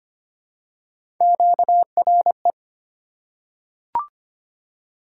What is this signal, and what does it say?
Morse code, heard as a keyed tone, about 25 wpm, 700 Hz. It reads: QRE
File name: Three Letter-Number Combinations - 25wpm